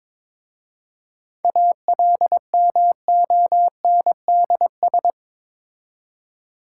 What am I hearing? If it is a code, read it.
Morse code, heard as a keyed tone, about 22 wpm, 690 Hz. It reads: ALMONDS
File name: Difficult Words with Hints - 22wpm